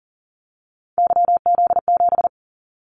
Morse code, heard as a keyed tone, about 40 wpm, 700 Hz. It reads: YZ7
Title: Callsign Prefixes for Top 25 DXCC - 40wpm